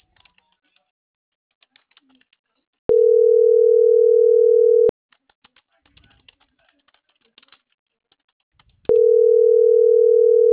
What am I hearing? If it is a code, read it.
Ringback tone (North America), 440 + 480 Hz.